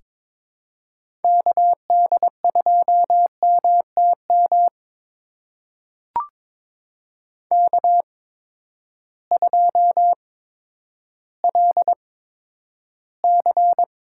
Morse code, heard as a keyed tone, about 22 wpm, spaced as for 11 wpm, 700 Hz. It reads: KD2MTM K2LC
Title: USA Call Signs - 8x Character Spacing - 22wpm